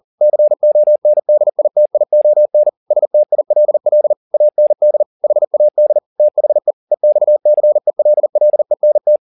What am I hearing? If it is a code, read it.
Morse code, heard as a keyed tone, about 40 wpm, 600 Hz. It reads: CONDITION STILL AND HAD THE EXCELLENT